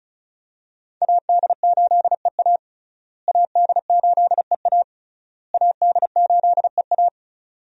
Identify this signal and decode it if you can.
Morse code, heard as a keyed tone, about 36 wpm, 700 Hz. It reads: AD8EA AD8EA AD8EA
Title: USA Call Signs - 5 Characters - Repeated 3x - 35wpm